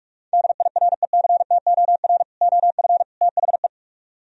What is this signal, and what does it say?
Morse code, heard as a keyed tone, about 44 wpm, 700 Hz. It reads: DIRECTOR OF THE